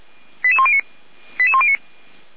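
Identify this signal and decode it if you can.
Five-tone selective call (ZVEI), heard as quick runs of five tones, tones of about 70 ms. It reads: 88109 88109
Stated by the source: Fire brigade ZVEI five-tone code 89080 received on an SDR, feuerwehr-orlamuende.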